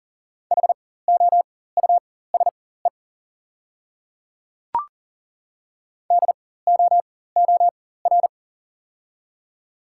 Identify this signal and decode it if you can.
Morse code, heard as a keyed tone, about 40 wpm, spaced as for 19 wpm, 690 Hz. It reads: HOUSE DOOR
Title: 4x Character Spacing - Top 500 Words - 40wpm